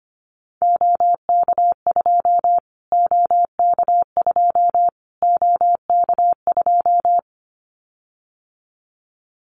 Morse code, heard as a keyed tone, about 25 wpm, 700 Hz. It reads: OK2 OK2 OK2
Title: Callsign Prefixes for Top 25 DXCC - Repeated 3x - 25wpm